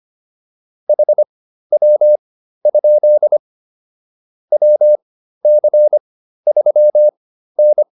Morse code, heard as a keyed tone, about 25 wpm, spaced as for 13 wpm, 600 Hz. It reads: HW? WC3N
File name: QSO209 13